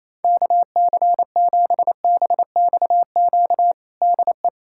Morse code, heard as a keyed tone, about 28 wpm, 700 Hz. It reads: KC7BXQ DE
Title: USA Callsign DE Callsign 28wpm